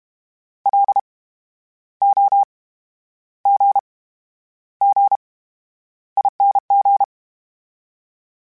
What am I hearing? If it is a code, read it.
Morse code, heard as a keyed tone, about 32 wpm, spaced as for 11 wpm, 800 Hz. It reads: LOGGING
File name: Special INGs / Special INGs-mp3s / VerySlowToFaster Ing @30